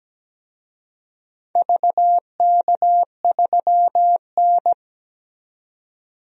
Morse code, heard as a keyed tone, about 17 wpm, 700 Hz. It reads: VK3N
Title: DXCC-100 Call Signs - 4 Characters - 17wpm